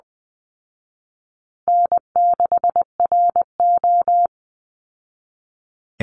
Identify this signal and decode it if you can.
Morse code, heard as a keyed tone, about 20 wpm, 700 Hz. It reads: N6RO
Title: Easy Call Signs Repeated 2x - 20wpm